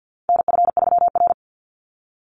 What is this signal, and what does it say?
Morse code, heard as a keyed tone, about 50 wpm, 700 Hz. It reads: NF3R
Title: Call Signs Repeated 2x - 50wpm